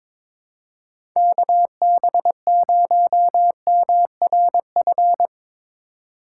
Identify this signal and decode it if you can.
Morse code, heard as a keyed tone, about 23 wpm, 700 Hz. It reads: KB0MRF
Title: USA Call Signs - 3x Character Spacing - 22wpm